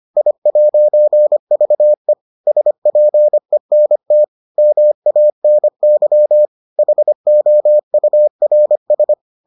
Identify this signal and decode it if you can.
Morse code, heard as a keyed tone, about 25 wpm, 600 Hz. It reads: I'VE SPENT MANY HOURS